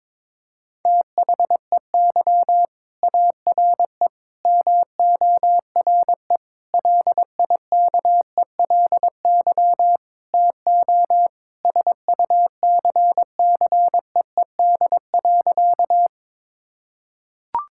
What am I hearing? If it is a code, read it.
Morse code, heard as a keyed tone, about 22 wpm, 700 Hz. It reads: THEY ARE MORE LIKELY TO SUCCEED.
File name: practice / 22wpm / Sentences with 4-Word Phrases - 22wpm